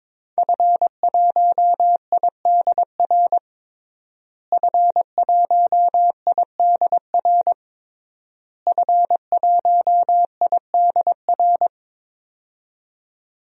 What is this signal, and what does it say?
Morse code, heard as a keyed tone, about 22 wpm, 700 Hz. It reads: F1IDR F1IDR F1IDR
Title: DXCC-100 Call Signs - 5 Characters - Repeated 3x - 22wpm